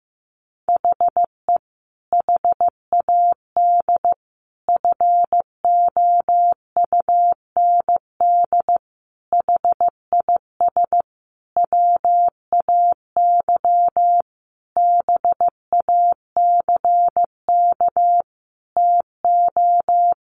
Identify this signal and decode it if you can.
Morse code, heard as a keyed tone, about 15 wpm, 700 Hz. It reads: HE HAD FOUND HIS WAY BACK TO
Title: A Sound Of Thunder 15wpm